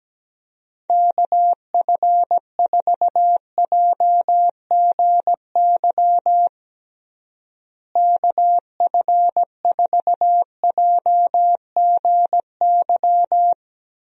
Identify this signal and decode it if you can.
Morse code, heard as a keyed tone, about 17 wpm, 700 Hz. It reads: KF4JGY KF4JGY
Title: USA Call Signs - 6 Characters - Repeated 2x - 17wpm